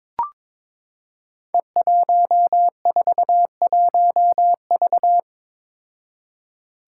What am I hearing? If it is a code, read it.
Morse code, heard as a keyed tone, about 22 wpm, 700 Hz. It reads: E141V
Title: Five Challenging Letter-Number Combinations 22wpm